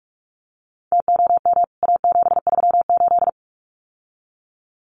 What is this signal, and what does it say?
Morse code, heard as a keyed tone, about 44 wpm, 700 Hz. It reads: TOM A738
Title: CWT Exchange 45wpm